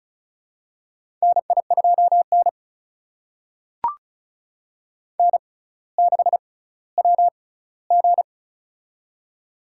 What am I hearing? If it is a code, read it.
Morse code, heard as a keyed tone, about 36 wpm, spaced as for 18 wpm, 700 Hz. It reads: NI2N N6WG
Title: USA Call Signs - 4 Characters - 6x Character Spacing - 35wpm